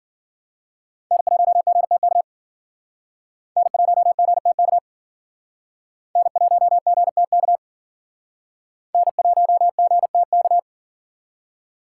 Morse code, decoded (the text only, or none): N1GTK N1GTK N1GTK N1GTK